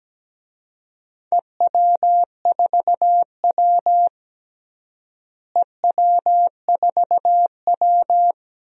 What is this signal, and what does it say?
Morse code, heard as a keyed tone, about 17 wpm, 700 Hz. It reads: EW4W EW4W
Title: DXCC-100 Call Signs - 4 Characters - Repeated 3x - 17wpm